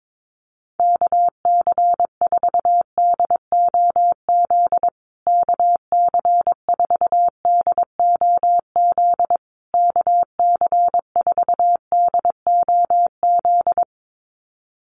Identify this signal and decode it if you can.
Morse code, heard as a keyed tone, about 22 wpm, 690 Hz. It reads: KC4DOZ KC4DOZ KC4DOZ
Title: USA Call Signs Repeated 3x 22wpm